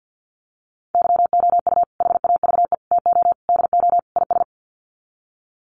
Morse code, heard as a keyed tone, about 50 wpm, 700 Hz. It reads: YOU HAVE TO DO IS
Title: Sets of 5 Words 50wpm